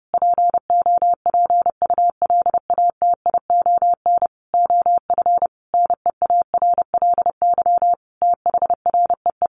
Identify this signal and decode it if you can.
Morse code, heard as a keyed tone, about 30 wpm, 700 Hz. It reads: POPULATION OF NEARLY THREE
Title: Sentences from Top 1300 Words 30wpm